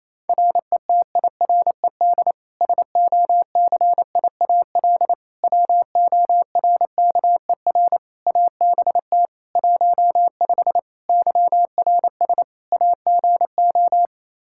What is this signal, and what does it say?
Morse code, heard as a keyed tone, about 27 wpm, 700 Hz. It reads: RETIRED SOCIAL WORKER ABT 15 YRS AGO
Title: K1USN SST QSO Practice 28wpm